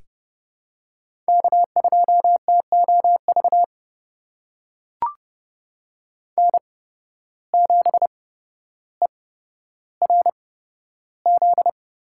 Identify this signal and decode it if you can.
Morse code, heard as a keyed tone, about 30 wpm, spaced as for 13 wpm, 700 Hz. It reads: K2TOV N7ERZ
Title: USA Call Signs - 5 Characters - 8x Character Spacing - 30wpm